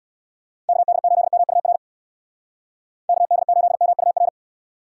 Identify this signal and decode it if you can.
Morse code, heard as a keyed tone, about 60 wpm, 700 Hz. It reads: KD8MUR KD8MUR
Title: Speed Racing - USA Call Signs 50wpm